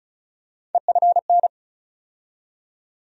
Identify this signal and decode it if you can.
Morse code, heard as a keyed tone, about 36 wpm, 700 Hz. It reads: EFN
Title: USA Call Sign Postfixes - 3 Letters 35wpm